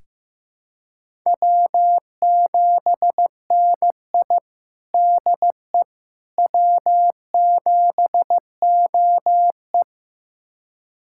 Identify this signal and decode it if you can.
Morse code, heard as a keyed tone, about 15 wpm, 700 Hz. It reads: W7NI DE W7OE
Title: USA Callsign DE Callsign - 4 Characters - 15wpm